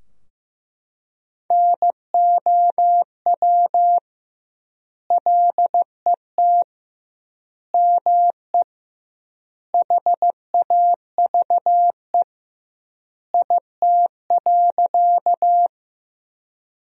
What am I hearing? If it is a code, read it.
Morse code, heard as a keyed tone, about 16 wpm, 700 Hz. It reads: NOW LET ME HAVE IT.
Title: 2x Word-Spacing - Sentences from Top 300 Words - 15wpm